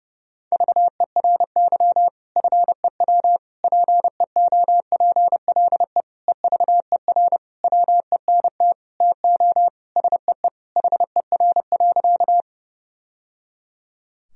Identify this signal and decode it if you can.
Morse code, heard as a keyed tone, about 32 wpm, 700 Hz. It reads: VERY FEW PEOPLE EVER WENT TO SEE HER.
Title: Sentences from Top 1000 Words - Review Part 1 - 30wpm